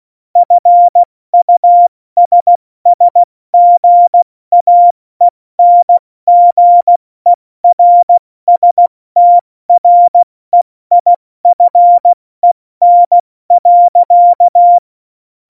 Morse code, heard as a keyed tone, about 16 wpm, 700 Hz. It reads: FUSSGAENGERSTREIFEN.
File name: Morsekurs / Witze / Witz 31. Oktober 17, Tempo 70